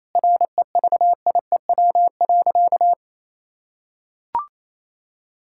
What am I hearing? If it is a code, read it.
Morse code, heard as a keyed tone, about 28 wpm, 700 Hz. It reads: REVIEW.
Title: Sentences from Top 1000 Words - Review Part 6 - 28wpm